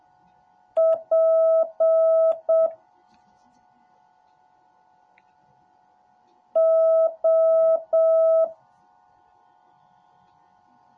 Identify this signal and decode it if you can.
Morse code, heard as a keyed tone, about 7 wpm, 650 Hz. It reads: P O